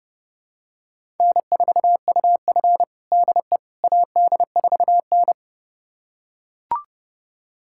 Morse code, heard as a keyed tone, about 30 wpm, 700 Hz. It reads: N4UF DE AD4N
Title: USA Callsign DE Callsign - 4 Characters - 30wpm